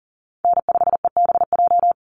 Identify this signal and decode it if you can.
Morse code, heard as a keyed tone, about 40 wpm, 700 Hz. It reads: N5EBJ
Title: USA Call Signs 40wpm